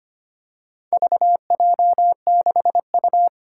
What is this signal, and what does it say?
Morse code, heard as a keyed tone, about 25 wpm, 700 Hz. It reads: VJ6U
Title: Four Challenging Letter-Number Combinations 25wpm